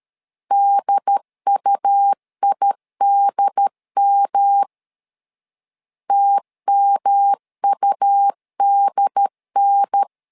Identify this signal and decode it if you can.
Morse code, heard as a keyed tone, about 13 wpm, 800 Hz. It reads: DUIDM TMUDN
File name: LEZIONI / 3 MWRO / 3 M 12wpm 1541